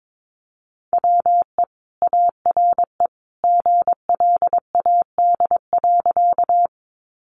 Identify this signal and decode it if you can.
Morse code, heard as a keyed tone, about 22 wpm, 700 Hz. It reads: WE ARE GLAD.
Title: Sentences from Top 800 Words 22wpm